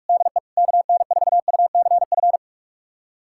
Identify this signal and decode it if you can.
Morse code, heard as a keyed tone, about 46 wpm, 700 Hz. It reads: DE KN4UCF
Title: USA Callsign DE Callsign 45wpm